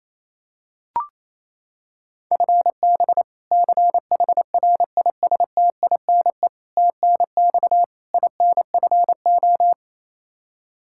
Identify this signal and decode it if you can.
Morse code, heard as a keyed tone, about 28 wpm, 700 Hz. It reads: FB CHRISTINE TNX INFO
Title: Practice QSO - Exchange 3 - More Info - 28wpm